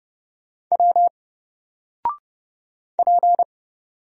Morse code, heard as a keyed tone, about 30 wpm, 700 Hz. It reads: W P
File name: Single-Letter-Number-30wpm